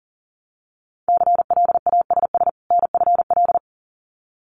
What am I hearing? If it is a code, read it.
Morse code, heard as a keyed tone, about 38 wpm, 700 Hz. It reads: CLASS NFL
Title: Winter Field Day Exchanges 40wpm